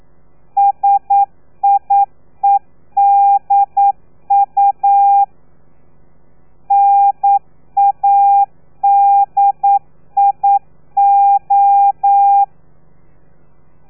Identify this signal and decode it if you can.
Morse code, heard as a keyed tone, about 9 wpm, 800 Hz. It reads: SIEDU NADIO
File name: telegrafia / i2tfj / Lez3 40